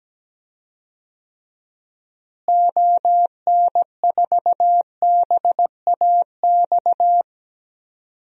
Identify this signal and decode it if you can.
Morse code, heard as a keyed tone, about 17 wpm, 700 Hz. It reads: ON4BAX
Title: DXCC-100 Call Signs - 6 Characters - Repeated 2x - 17wpm